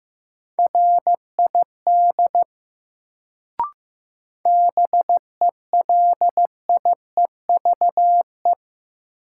Morse code, heard as a keyed tone, about 15 wpm, 700 Hz. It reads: RID BELIEVE
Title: Sets of 4 Words - New Word Drill - 15wpm